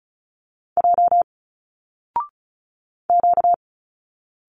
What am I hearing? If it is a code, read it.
Morse code, heard as a keyed tone, about 36 wpm, 700 Hz. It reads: J Q
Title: Single Letters 35wpm